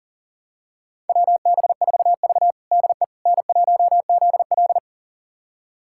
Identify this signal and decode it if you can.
Morse code, heard as a keyed tone, about 40 wpm, 700 Hz. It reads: WB4V DE N1ZL
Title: USA Callsign DE Callsign - 4 Characters - 40wpm